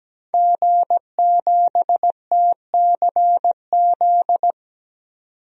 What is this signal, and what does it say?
Morse code, heard as a keyed tone, about 17 wpm, 700 Hz. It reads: G7TCZ
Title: DXCC-100 Call Signs - 5 Characters - 17wpm